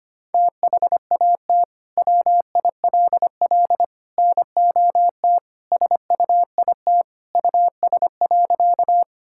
Morse code, heard as a keyed tone, about 25 wpm, 700 Hz. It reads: THAT WILL NOT SUIT US.
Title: Sentences from Top 1600 Words 25wpm